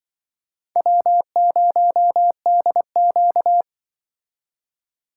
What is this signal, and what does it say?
Morse code, heard as a keyed tone, about 25 wpm, 700 Hz. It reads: W0DQ
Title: Speed Racing - USA 1x2 Call Signs 20wpm